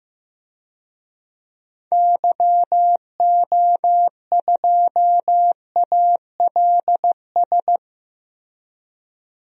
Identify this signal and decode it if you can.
Morse code, heard as a keyed tone, about 16 wpm, 700 Hz. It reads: YO2ALS
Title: DXCC-100 Call Signs - 6 Characters - Repeated 3x - 15wpm